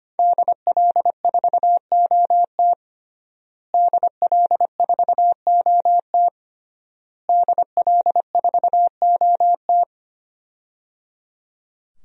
Morse code, heard as a keyed tone, about 25 wpm, 700 Hz. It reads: DL4OT DL4OT DL4OT
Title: USA Call Signs - 5 Characters - Repeated 3x - 25wpm